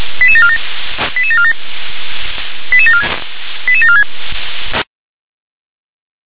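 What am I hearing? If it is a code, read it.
Five-tone selective call (ZVEI), heard as quick runs of five tones, tones of about 70 ms. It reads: 88747 88747 88746
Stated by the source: ZVEI five-tone selective call 88148 received on an SDR.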